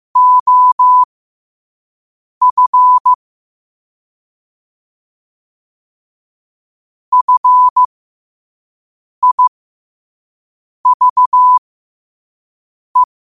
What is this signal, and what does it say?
Morse code, heard as a keyed tone, about 15 wpm, spaced as for 5 wpm, 1000 Hz. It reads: OF FIVE
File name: qotdfiles / 5wpm